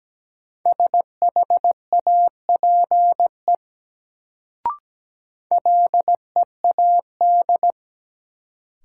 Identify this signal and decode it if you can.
Morse code, heard as a keyed tone, about 17 wpm, 700 Hz. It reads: SHAPE LEAD
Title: Sentences from Top 500 Words - New Word Drill - 17wpm